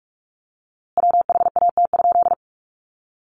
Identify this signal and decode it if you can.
Morse code, heard as a keyed tone, about 44 wpm, 700 Hz. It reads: WHAT?
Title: A Sound Of Thunder 45wpm